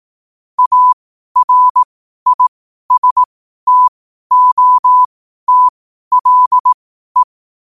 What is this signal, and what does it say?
Morse code, heard as a keyed tone, about 18 wpm, spaced as for 12 wpm, 1000 Hz. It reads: ARISTOTLE
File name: qotdfiles / 12wpm